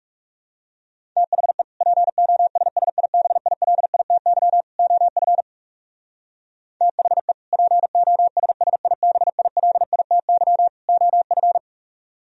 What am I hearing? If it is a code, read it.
Morse code, heard as a keyed tone, about 37 wpm, 700 Hz. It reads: THE POSSIBILITY OF THE POSSIBILITY OF
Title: Speed-Racing - Sets of 3 Words 30wpm